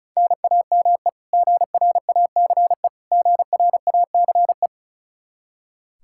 Morse code, heard as a keyed tone, about 36 wpm, 700 Hz. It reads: NAME GRACE GRACE
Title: Practice QSO - Exchange 1 - Beginning - 35wpm